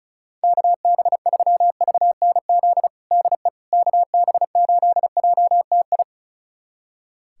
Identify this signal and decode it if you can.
Morse code, heard as a keyed tone, about 34 wpm, 700 Hz. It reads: KB3VNZ DE KB8JTI
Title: USA Callsign DE Callsign 35wpm